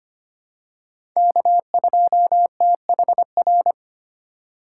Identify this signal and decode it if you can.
Morse code, heard as a keyed tone, about 25 wpm, 700 Hz. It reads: K2THR
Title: USA Call Signs - 2x Character Spacing - 25wpm